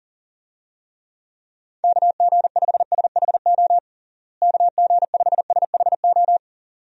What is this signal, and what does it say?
Morse code, heard as a keyed tone, about 37 wpm, 700 Hz. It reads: KG5SHO KG5SHO
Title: USA Call Signs - 6 Characters - Repeated 2x - 40wpm